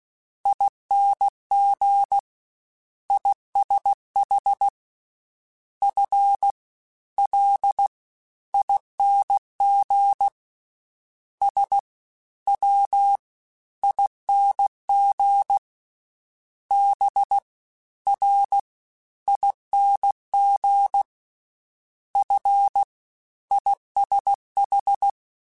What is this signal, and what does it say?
Morse code, heard as a keyed tone, about 16 wpm, spaced as for 12 wpm, 800 Hz. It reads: INGISH FLING SWING BRING FISH
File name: Misc / Misc-mp3s / Jumps @15